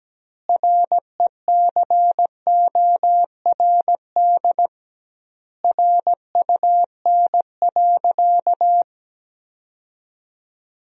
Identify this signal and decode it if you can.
Morse code, heard as a keyed tone, about 17 wpm, 700 Hz. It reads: RECORD RUN.
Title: 2x Word-Spacing - Sentences from Top 500 Words - 17wpm